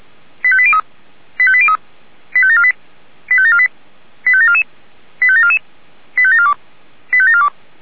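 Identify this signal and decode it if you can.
Five-tone selective call (ZVEI), heard as quick runs of five tones, tones of about 70 ms. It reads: 85803 85803 85748 85748 85744 85744 85742 85742
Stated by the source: ZVEI five-tone call 85753 received by SDR.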